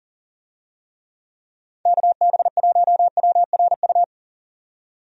Morse code, heard as a keyed tone, about 40 wpm, 700 Hz. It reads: KB1WRU
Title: USA Call Signs - 6 Characters - 40wpm